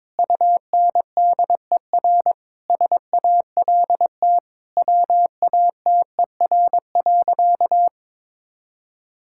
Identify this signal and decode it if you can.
Morse code, heard as a keyed tone, about 22 wpm, 700 Hz. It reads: UNDER SALT WATER.
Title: Sentences from Top 1000 Words - Review Part 5 - 22wpm